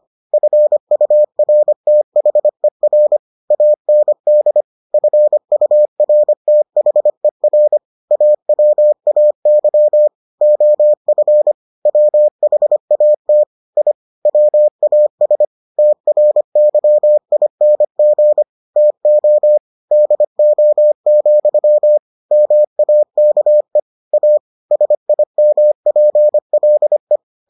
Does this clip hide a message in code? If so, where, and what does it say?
Yes: Morse code from 0.3 to 27.2 s, heard as a keyed tone, about 25 wpm, 600 Hz. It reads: FURTHER AND FURTHER AWAY OF WHAT I WAS TRYING TO DO, MAKE A SIMPLE